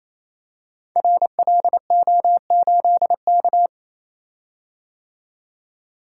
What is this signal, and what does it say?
Morse code, heard as a keyed tone, about 28 wpm, 700 Hz. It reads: RLO8K
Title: Code Groups - 5 Letter:Numbers - 28wpm